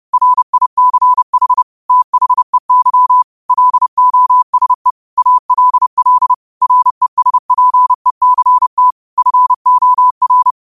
Morse code, heard as a keyed tone, about 30 wpm, 1000 Hz. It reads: RICH THEY LOSE ALL RESPECT FOR